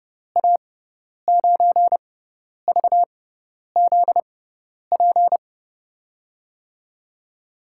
Morse code, heard as a keyed tone, about 30 wpm, spaced as for 10 wpm, 700 Hz. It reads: A9VZP
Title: USA Call Signs - 6 Characters - 6x Character Spacing - 30wpm